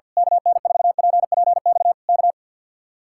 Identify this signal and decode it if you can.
Morse code, heard as a keyed tone, about 48 wpm, 700 Hz. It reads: KN4PPX K
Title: Speed-Racing Common QSO Phrases - 50wpm